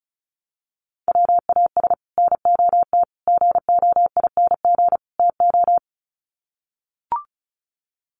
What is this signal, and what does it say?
Morse code, heard as a keyed tone, about 34 wpm, 700 Hz. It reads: WAS NOT GOING TO
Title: Sets of 4 Words 35wpm